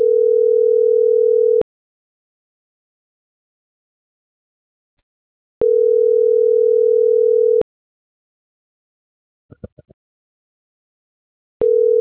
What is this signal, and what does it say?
Ringback tone (North America), 440 + 480 Hz.
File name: CENTRAL SAGOT / out adriana.poncec